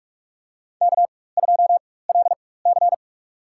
Morse code, heard as a keyed tone, about 47 wpm, spaced as for 21 wpm, 700 Hz. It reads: K2LC
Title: USA Call Signs - 4 Characters - 4x Character Spacing - 45wpm